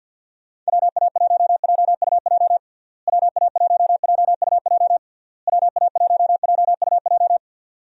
Morse code, heard as a keyed tone, about 50 wpm, 700 Hz. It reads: WA1JUJ WA1JUJ WA1JUJ
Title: USA Call Signs - 6 Characters - Repeated 3x - 50wpm